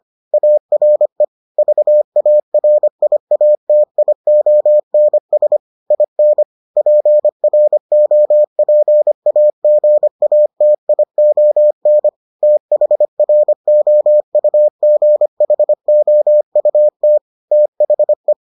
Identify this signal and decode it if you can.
Morse code, heard as a keyed tone, about 25 wpm, 600 Hz. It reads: ARE VARIATIONS IN PROPAGATION THROUGHOUT THE